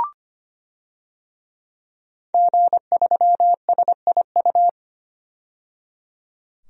Morse code, heard as a keyed tone, about 25 wpm, 700 Hz. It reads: G3SIU
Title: DXCC-100 Call Signs - 5 Characters - 25wpm